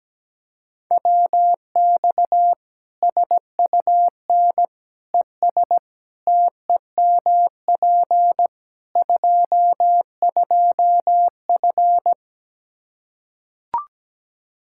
Morse code, heard as a keyed tone, about 17 wpm, 700 Hz. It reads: WX SUN ES TEMP 22F
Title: Practice QSO - Exchange 2 - Info - 17wpm